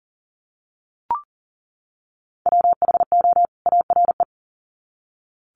Morse code, heard as a keyed tone, about 40 wpm, 700 Hz. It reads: WHO ARE
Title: Sets of 2 Words 40wpm